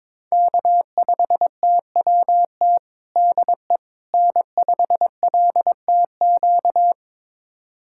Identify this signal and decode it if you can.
Morse code, heard as a keyed tone, about 22 wpm, 700 Hz. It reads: K5TWT DE N5LTQ
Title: USA Callsign DE Callsign - 5 Characters - 22wpm